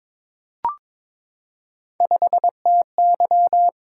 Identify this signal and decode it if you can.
Morse code, heard as a keyed tone, about 22 wpm, 700 Hz. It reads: 5TY
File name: practice / 22wpm / Three Letter-Number Combinations - 22wpm